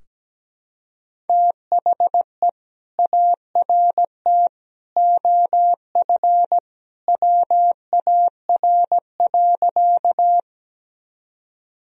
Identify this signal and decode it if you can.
Morse code, heard as a keyed tone, about 17 wpm, 700 Hz. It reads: THE ART OF WAR.
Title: Sentences from Top 1000 Words - Review Part 3 - 17wpm